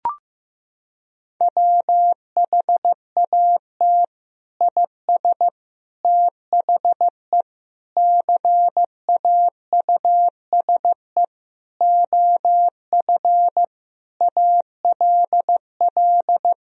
Morse code, heard as a keyed tone, about 15 wpm, 700 Hz. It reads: WHAT IS THE CAUSE OF ALL
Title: Sentences from Top 1000 Words - Review Part 1 - 15wpm